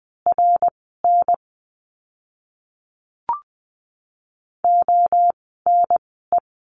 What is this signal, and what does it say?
Morse code, heard as a keyed tone, about 20 wpm, spaced as for 14 wpm, 700 Hz. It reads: RN ONE
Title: 185 Lesson 30f - Letter b - Words 20wpm